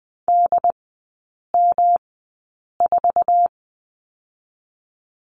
Morse code, heard as a keyed tone, about 20 wpm, spaced as for 8 wpm, 700 Hz. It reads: DM4
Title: 121 Lesson 20e - Number 3 - Three Characters 20wpm